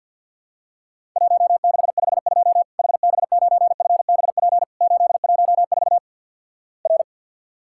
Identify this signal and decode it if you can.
Morse code, heard as a keyed tone, about 50 wpm, 700 Hz. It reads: 1652 HB9FBP 814 R
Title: Worked All Europe DX Contest - QTC Exchange - 50wpm